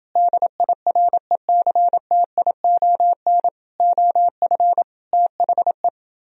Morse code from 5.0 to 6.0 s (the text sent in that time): THE